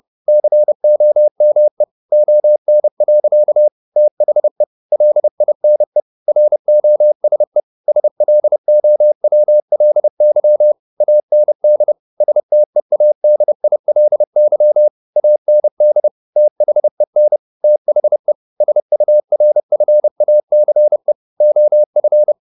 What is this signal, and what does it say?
Morse code, heard as a keyed tone, about 30 wpm, 600 Hz. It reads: COME ON. THE LINE ROSE SLOWLY AND STEADILY AND THEN THE SURFACE OF
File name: the old man and the sea / the old man and the sea 1 30wpm